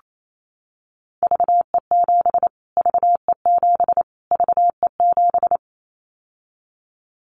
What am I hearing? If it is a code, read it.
Morse code, heard as a keyed tone, about 28 wpm, 700 Hz. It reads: VE7 VE7 VE7
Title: Callsign Prefixes for Top 25 DXCC - Repeated 3x - 28wpm